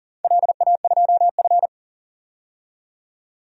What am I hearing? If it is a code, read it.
Morse code, heard as a keyed tone, about 42 wpm, 700 Hz. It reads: LA2F
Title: DXCC-100 Call Signs - 4 Characters - 40wpm